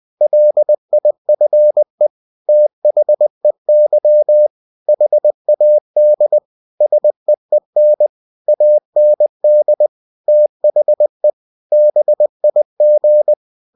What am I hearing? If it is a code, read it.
Morse code, heard as a keyed tone, about 20 wpm, 600 Hz. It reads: LIFE THEY HAD SEEN AND THE BIG
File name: cw-books / winnie-the-pooh / winnie-20wpm-page-0101